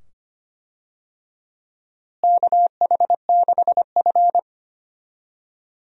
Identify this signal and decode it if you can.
Morse code, heard as a keyed tone, about 25 wpm, 700 Hz. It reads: KH6F
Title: USA Call Signs - 4 Characters - Repeated 2x - 25wpm